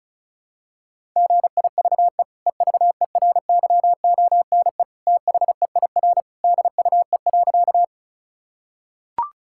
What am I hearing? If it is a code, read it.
Morse code, heard as a keyed tone, about 34 wpm, 700 Hz. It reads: GIVE EVERYONE THEIR DUE.
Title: Sentences from Top 1000 Words - Review Part 6 - 35wpm